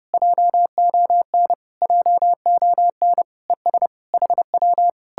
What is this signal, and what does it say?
Morse code, heard as a keyed tone, about 33 wpm, 700 Hz. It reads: JON JON ES HW